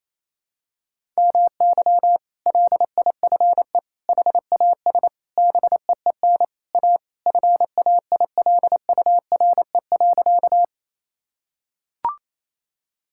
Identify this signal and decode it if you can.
Morse code, heard as a keyed tone, about 28 wpm, 700 Hz. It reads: MY LIFE HAS BEEN A FAILURE.
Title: Sentences from Top 1000 Words - Review Part 6 - 28wpm